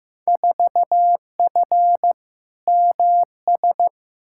Morse code, heard as a keyed tone, about 15 wpm, 700 Hz. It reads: 4F MS
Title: Field Day Exchanges Encore - 15wpm